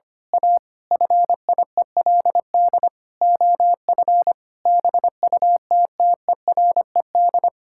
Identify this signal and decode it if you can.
Morse code, heard as a keyed tone, about 25 wpm, 700 Hz. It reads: A FIELD OF BUTTERED
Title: Pangram Challenge v2 - 25wpm